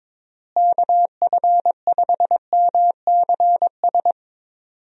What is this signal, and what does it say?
Morse code, heard as a keyed tone, about 22 wpm, 700 Hz. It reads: KF5MCS
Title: USA Call Signs - 6 Characters - Repeated 2x - 22wpm